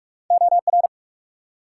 Morse code, heard as a keyed tone, about 63 wpm, spaced as for 54 wpm, 700 Hz. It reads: OR
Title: Speed-Racing Multi-letter Phonemes - 45wpm